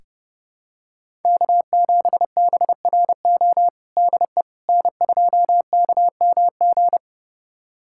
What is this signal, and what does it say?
Morse code, heard as a keyed tone, about 30 wpm, 700 Hz. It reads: K7BRO DE N2KMG
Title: USA Callsign DE Callsign - 5 Characters - 30wpm